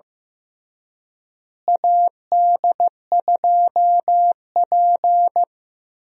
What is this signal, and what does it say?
Morse code, heard as a keyed tone, about 15 wpm, 700 Hz. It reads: AD2P
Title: USA Call Signs - 4 Characters - Repeated 3x - 15wpm